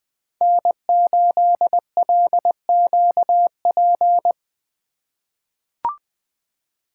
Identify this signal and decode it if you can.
Morse code, heard as a keyed tone, about 20 wpm, 700 Hz. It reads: N8LQP
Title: USA Call Signs - 5 Characters - 3x Character Spacing - 20wpm